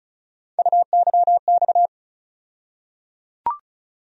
Morse code, heard as a keyed tone, about 36 wpm, 700 Hz. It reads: UYX
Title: USA Call Sign Postfixes - 3 Letters 35wpm